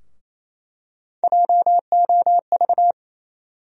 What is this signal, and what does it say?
Morse code, heard as a keyed tone, about 28 wpm, 700 Hz. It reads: JOV
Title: USA Call Sign Postfixes - 3 Letters 28wpm